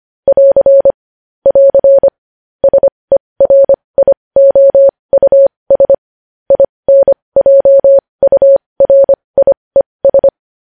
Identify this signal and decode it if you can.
Morse code, heard as a keyed tone, about 25 wpm, spaced as for 20 wpm, 550 Hz. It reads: <AR> <AR> SERIOUS INJURIES
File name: podcast / News headlines-1x-25x20